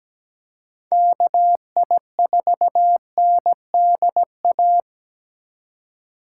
Morse code, heard as a keyed tone, about 17 wpm, 700 Hz. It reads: KI4NDA
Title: USA Call Signs - 6 Characters - 4x Character Spacing - 17wpm